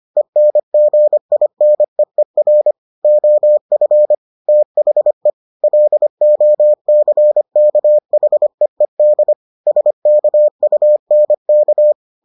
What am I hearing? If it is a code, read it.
Morse code, heard as a keyed tone, about 25 wpm, 600 Hz. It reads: ENGINEER OF THE LOCKHEED SKUNK